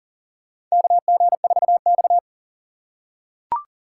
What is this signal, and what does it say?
Morse code, heard as a keyed tone, about 40 wpm, 700 Hz. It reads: KG4X
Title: USA Call Signs - 6x Character Spacing - 40wpm